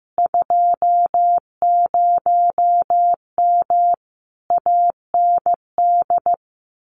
Morse code, heard as a keyed tone, about 15 wpm, 700 Hz. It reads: 20M AND
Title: Common QSO Phrases Encore 15wpm